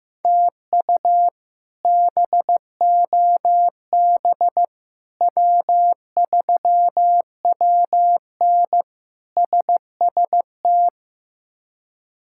Morse code, heard as a keyed tone, about 15 wpm, 700 Hz. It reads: TU BOB W3WN SST
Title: K1USN SST QSO Practice - 15wpm